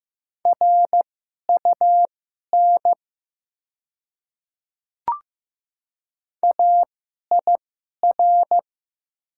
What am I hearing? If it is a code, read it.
Morse code, heard as a keyed tone, about 15 wpm, spaced as for 11 wpm, 700 Hz. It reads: RUN AIR
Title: 2x Character Spacing - Top 300 Words - 15wpm